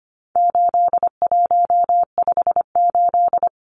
Morse code, heard as a keyed tone, about 25 wpm, 700 Hz. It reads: 8158
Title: Numbers - 4-Digit 25wpm